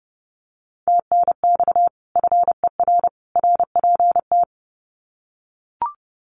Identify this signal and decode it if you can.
Morse code, heard as a keyed tone, about 30 wpm, 700 Hz. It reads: TNX FER RPT
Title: Common QSO Phrases 30wpm